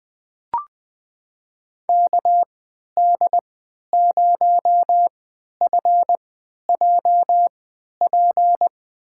Morse code, heard as a keyed tone, about 21 wpm, spaced as for 11 wpm, 700 Hz. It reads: KD0FJP
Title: USA Call Signs - 6 Characters - 3x Character Spacing - 20wpm